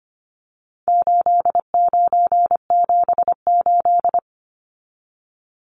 Morse code, heard as a keyed tone, about 25 wpm, 700 Hz. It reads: 8978
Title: Numbers - 4-Digit 25wpm